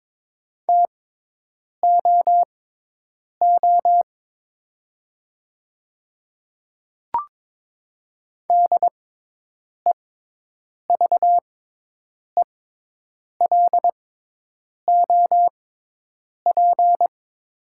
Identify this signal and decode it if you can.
Morse code, heard as a keyed tone, about 22 wpm, spaced as for 8 wpm, 700 Hz. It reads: TOO DEVELOP
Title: 6x Character Spacing - Top 500 Words - 22wpm